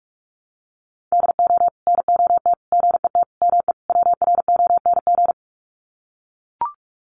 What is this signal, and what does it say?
Morse code, heard as a keyed tone, about 44 wpm, 700 Hz. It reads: DO NOT GET ME WRONG
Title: Sets of 5 Words Encore 45wpm